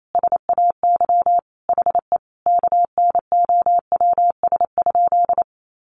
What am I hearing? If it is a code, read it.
Morse code, heard as a keyed tone, about 28 wpm, 700 Hz. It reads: SAY HE KNOWS?
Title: A Sound Of Thunder 28wpm